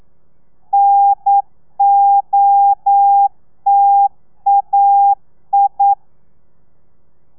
Morse code, heard as a keyed tone, about 9 wpm, 800 Hz. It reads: NOTAI
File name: telegrafia / i2tfj / Lez2 40